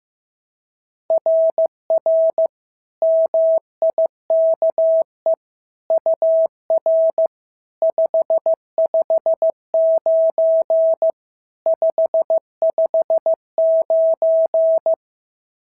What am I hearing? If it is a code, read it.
Morse code, heard as a keyed tone, about 14 wpm, 650 Hz. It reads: RR MIKE UR 559 559
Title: SOTA QSO Practice 15wpm